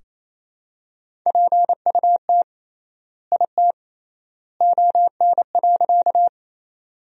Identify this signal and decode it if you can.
Morse code, heard as a keyed tone, about 27 wpm, 700 Hz. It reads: PUT IT ON.
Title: 3x Word-Spacing - Sentences from Top 200 Words - 28wpm